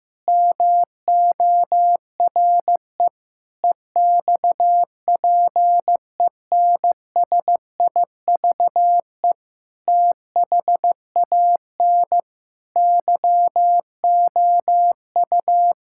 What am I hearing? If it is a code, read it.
Morse code, heard as a keyed tone, about 15 wpm, 700 Hz. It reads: MORE EXPENSIVE THAN YOU
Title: Sentences from Top 1000 Words - Review Part 6 - 15wpm